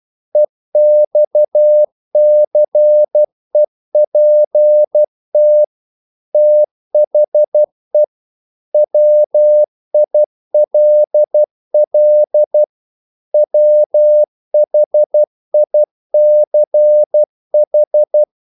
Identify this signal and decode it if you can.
Morse code, heard as a keyed tone, about 12 wpm, 600 Hz. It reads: EXCEPT THE WILL WHICH
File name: CW / Practice / If12